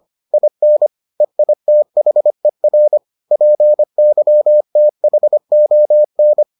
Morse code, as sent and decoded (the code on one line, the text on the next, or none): .. -. / . .. - .... .-. / .--. -.-- - .... --- -.
IN EITHER PYTHON